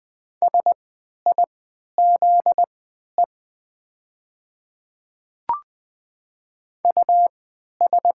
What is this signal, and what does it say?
Morse code, heard as a keyed tone, about 20 wpm, spaced as for 11 wpm, 690 Hz. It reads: SIZE US
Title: 3x Character Spacing - Top 500 Words - 20wpm